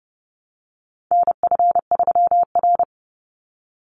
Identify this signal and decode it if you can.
Morse code, heard as a keyed tone, about 30 wpm, 700 Hz. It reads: NF3R
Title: Call Signs Repeated 2x - 30wpm